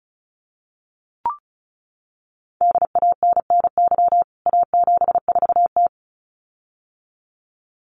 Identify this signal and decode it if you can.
Morse code, heard as a keyed tone, about 34 wpm, 700 Hz. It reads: DANNY A74T
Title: CWT Exchange 35wpm